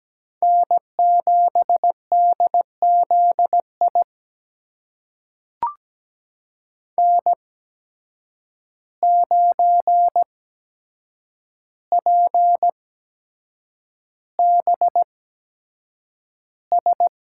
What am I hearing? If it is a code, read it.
Morse code, heard as a keyed tone, about 17 wpm, spaced as for 7 wpm, 700 Hz. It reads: N7DZI N9PBS
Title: USA Call Signs - 5 Characters - 8x Character Spacing - 17wpm